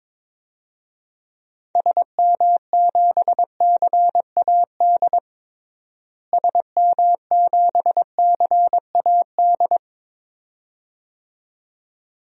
Morse code, heard as a keyed tone, about 21 wpm, 700 Hz. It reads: SM7CAD SM7CAD
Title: DXCC-100 Call Signs - 6 Characters - Repeated 2x - 22wpm